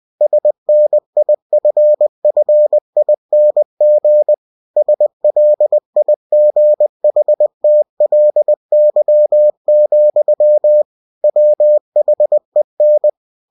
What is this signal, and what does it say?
Morse code, heard as a keyed tone, about 20 wpm, 600 Hz. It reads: SNIFFING SLIGHTLY, WHEN
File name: cw-books / winnie-the-pooh / winnie-20wpm-page-0062